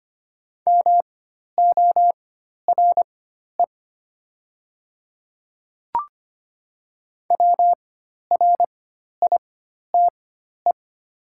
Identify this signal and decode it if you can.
Morse code, heard as a keyed tone, about 25 wpm, spaced as for 12 wpm, 700 Hz. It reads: MORE WRITE
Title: 4x Character Spacing - Top 100 Words - 25wpm